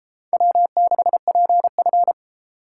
Morse code, heard as a keyed tone, about 31 wpm, 700 Hz. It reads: W6PF